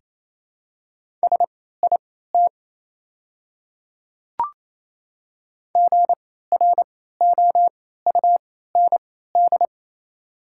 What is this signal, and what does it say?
Morse code, heard as a keyed tone, about 28 wpm, spaced as for 16 wpm, 690 Hz. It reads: SIT GROUND
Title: 3x Character Spacing - Top 500 Words - 28wpm